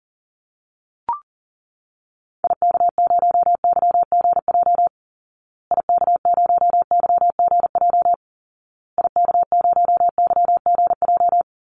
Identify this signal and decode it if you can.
Morse code, heard as a keyed tone, about 40 wpm, 700 Hz. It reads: IK0YGJ IK0YGJ IK0YGJ
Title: Call Signs Repeated 2x - 40wpm